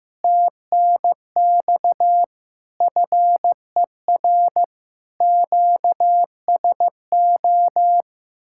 Morse code, heard as a keyed tone, about 15 wpm, 700 Hz. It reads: TNX FER QSO
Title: Speed-Racing Common QSO Phrases Encore - 15wpm